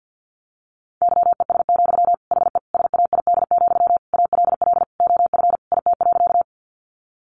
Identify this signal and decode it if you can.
Morse code, heard as a keyed tone, about 50 wpm, 700 Hz. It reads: YES, HE SAID, ALL OF IT.
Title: Sentences from Top 500 Words 50wpm